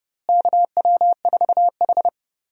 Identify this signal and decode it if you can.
Morse code, heard as a keyed tone, about 30 wpm, 700 Hz. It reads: KW4H
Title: USA Call Signs - 4 Characters - Repeated 3x - 30wpm